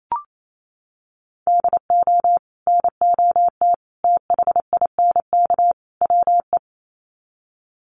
Morse code, heard as a keyed tone, about 28 wpm, 700 Hz. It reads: DO NOT THINK WE
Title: Sets of 4 Words 28wpm